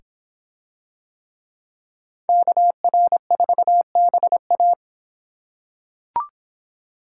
Morse code, heard as a keyed tone, about 23 wpm, 700 Hz. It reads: KR4BA